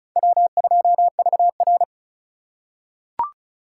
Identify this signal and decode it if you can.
Morse code, heard as a keyed tone, about 36 wpm, 700 Hz. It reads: W2VR
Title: USA Call Signs - 4 Characters - 3x Character Spacing - 35wpm